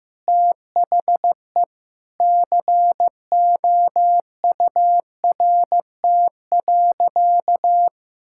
Morse code, heard as a keyed tone, about 15 wpm, 700 Hz. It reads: THE COURT.
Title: Sentences from Top 1600 Words 15wpm